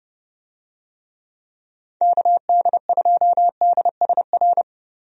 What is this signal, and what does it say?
Morse code, heard as a keyed tone, about 32 wpm, 700 Hz. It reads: KD2DSR
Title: USA Call Signs - 6 Characters - 30wpm